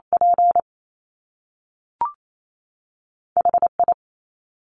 Morse code, heard as a keyed tone, about 28 wpm, 700 Hz. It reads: P H I
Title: Top Two Letter Combinations in Words 28wpm